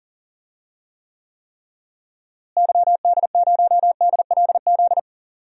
Morse code, heard as a keyed tone, about 40 wpm, 700 Hz. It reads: YD0DLZ
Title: DXCC-100 Call Signs - 6 Characters - Repeated 3x - 40wpm